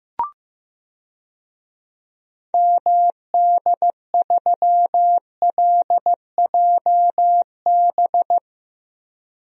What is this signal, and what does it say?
Morse code, heard as a keyed tone, about 15 wpm, 700 Hz. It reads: MD3LJB
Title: DXCC-100 Call Signs - 6 Characters - 15wpm